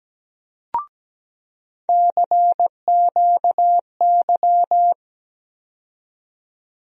Morse code, heard as a keyed tone, about 17 wpm, 700 Hz. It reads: CQY
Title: USA Call Sign Postfixes - 3 Letters 17wpm